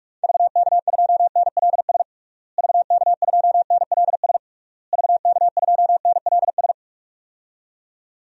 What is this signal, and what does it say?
Morse code, heard as a keyed tone, about 46 wpm, 700 Hz. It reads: VK2NLS VK2NLS VK2NLS
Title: DXCC-100 Call Signs - 6 Characters - Repeated 3x - 45wpm